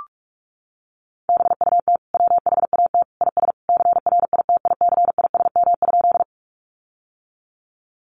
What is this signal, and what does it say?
Morse code, heard as a keyed tone, about 44 wpm, 700 Hz. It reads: BUT WHAT IS CRITICISM?
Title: Sentences from Top 1300 Words 45wpm